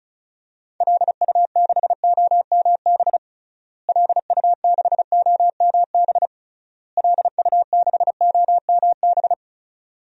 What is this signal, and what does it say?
Morse code, heard as a keyed tone, about 36 wpm, 700 Hz. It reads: LU6OMB LU6OMB LU6OMB
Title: DXCC-100 Call Signs - 6 Characters - Repeated 3x - 35wpm